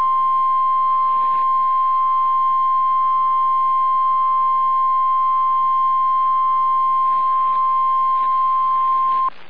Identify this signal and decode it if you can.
Emergency Alert System (SAME) tones (the 1050 Hz attention tone). It is heard from 0.9 to 8.9 s.